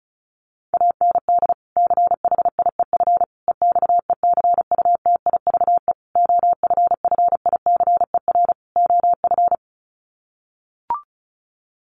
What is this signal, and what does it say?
Morse code, heard as a keyed tone, about 34 wpm, 700 Hz. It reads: AND CHIEF EXECUTIVE OFFICER OF
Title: Sets of 5 Words Encore2 35wpm